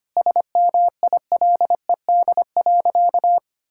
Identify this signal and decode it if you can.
Morse code, heard as a keyed tone, about 25 wpm, 700 Hz. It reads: SMILED.
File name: practice / 25wpm / Sentences with 4-Word Phrases - 25wpm